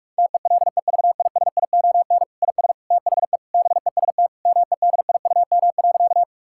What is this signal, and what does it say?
Morse code, heard as a keyed tone, about 46 wpm, 700 Hz. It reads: TELEVISION IS THE BEST MEDIUM.
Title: Sentences from Top 1000 Words - Review Part 6 - 45wpm